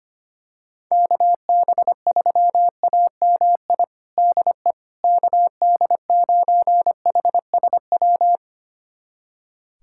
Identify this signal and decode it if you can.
Morse code, heard as a keyed tone, about 25 wpm, 700 Hz. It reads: KB3AMI DE KD9HSW
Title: USA Callsign DE Callsign - 6 Characters - 25wpm